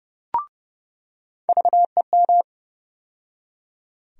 Morse code, heard as a keyed tone, about 30 wpm, 700 Hz. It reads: VEM
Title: USA Call Sign Postfixes - 3 Letters 30wpm